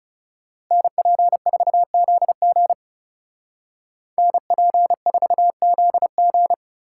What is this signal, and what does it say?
Morse code, heard as a keyed tone, about 36 wpm, 700 Hz. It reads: NP4ZG NP4ZG
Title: Speed Racing - USA 2x2 Call Signs 30wpm